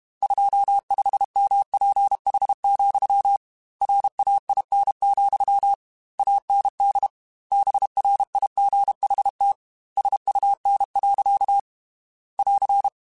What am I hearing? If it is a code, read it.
Morse code, heard as a keyed tone, about 32 wpm, 800 Hz. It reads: 25MPH, RAIN, AND BRIGHT SUN. <AR>
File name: Weather Sets / Weather Sets-mp3s / Weather 5 5 @30